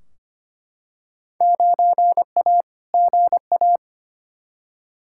Morse code, heard as a keyed tone, about 25 wpm, 700 Hz. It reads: 9A GA
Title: Field Day Exchanges Encore - 25wpm